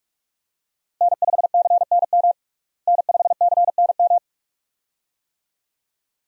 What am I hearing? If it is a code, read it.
Morse code, heard as a keyed tone, about 46 wpm, 700 Hz. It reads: N5CNM N5CNM
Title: USA Call Signs - 5 Characters - Repeated 2x - 45wpm